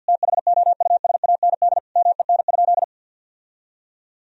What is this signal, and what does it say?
Morse code, heard as a keyed tone, about 50 wpm, 700 Hz. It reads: THOUSAND MEN?
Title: Sentences from Top 1000 Words - Review Part 3 - 50wpm